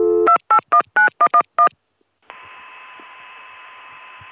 DTMF, heard as telephone touch tones, tones of about 90 ms. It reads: 2819111